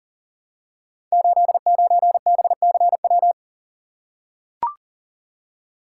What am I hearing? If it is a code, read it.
Morse code, heard as a keyed tone, about 38 wpm, 700 Hz. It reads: 89BCW
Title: Code Groups - 5 Letter:Numbers - 40wpm